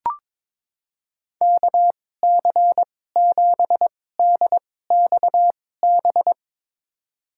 Morse code, heard as a keyed tone, about 22 wpm, spaced as for 16 wpm, 700 Hz. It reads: KC7DXB
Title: USA Call Signs - 6 Characters - 2x Character Spacing - 22wpm